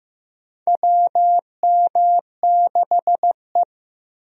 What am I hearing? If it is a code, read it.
Morse code, heard as a keyed tone, about 15 wpm, 700 Hz. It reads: WM6E
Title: USA Call Signs - 4 Characters - Repeated 2x - 15wpm